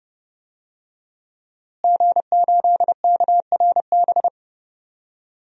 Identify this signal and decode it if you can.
Morse code, heard as a keyed tone, about 32 wpm, 700 Hz. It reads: G8KRB
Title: DXCC-100 Call Signs - 5 Characters - Repeated 2x - 30wpm